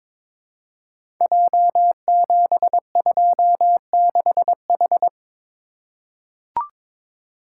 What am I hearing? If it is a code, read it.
Morse code, heard as a keyed tone, about 22 wpm, 700 Hz. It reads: J726H
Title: Code Groups - 5 Letter:Numbers - 22wpm